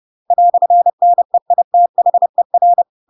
Morse code, heard as a keyed tone, about 30 wpm, 700 Hz. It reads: "NEITHER
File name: Pinocchio / Pinocchio ch8 30 30 ik1hge